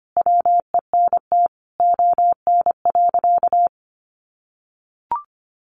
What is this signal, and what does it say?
Morse code, heard as a keyed tone, about 25 wpm, 700 Hz. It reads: WENT ON.
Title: Sentences from Top 600 Words 25wpm